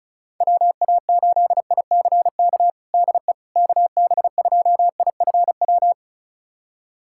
Morse code, heard as a keyed tone, about 36 wpm, 700 Hz. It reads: WA8ICK DE KB2IFW
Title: USA Callsign DE Callsign 35wpm